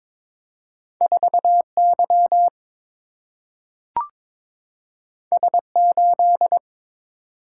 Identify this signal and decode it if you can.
Morse code, heard as a keyed tone, about 22 wpm, 700 Hz. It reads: 4 Y S 8
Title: Single-Letter Single-Number 22wpm